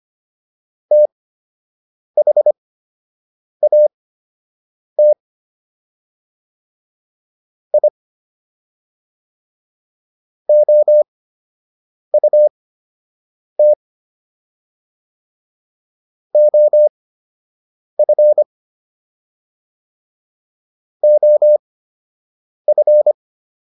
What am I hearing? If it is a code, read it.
Morse code, heard as a keyed tone, about 25 wpm, spaced as for 7 wpm, 600 Hz. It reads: THAT I OUT OF OF